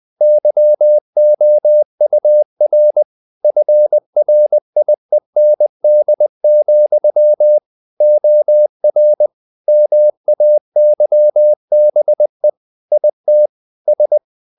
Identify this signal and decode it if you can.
Morse code, heard as a keyed tone, about 20 wpm, 600 Hz. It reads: YOUR FRIEND, OR MAYBE IT S
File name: the old man and the sea part 2 20wpm